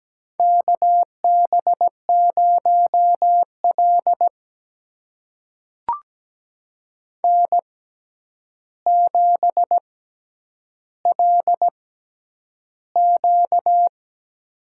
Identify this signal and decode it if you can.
Morse code, heard as a keyed tone, about 17 wpm, spaced as for 9 wpm, 700 Hz. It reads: KB0L N7LQ
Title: USA Call Signs - 4 Characters - 6x Character Spacing - 17wpm